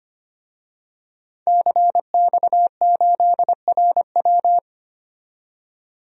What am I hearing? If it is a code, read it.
Morse code, heard as a keyed tone, about 25 wpm, 700 Hz. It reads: CX8RW
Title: DXCC-100 Call Signs - 5 Characters - 25wpm